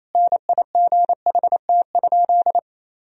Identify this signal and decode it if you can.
Morse code, heard as a keyed tone, about 28 wpm, 700 Hz. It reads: NIGHT?
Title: Sentences from Top 1000 Words - Review Part 4 - 28wpm